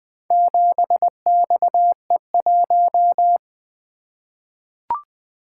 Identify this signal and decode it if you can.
Morse code, heard as a keyed tone, about 20 wpm, 700 Hz. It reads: 7XE1
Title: Four Letter-Number Combinations - 20wpm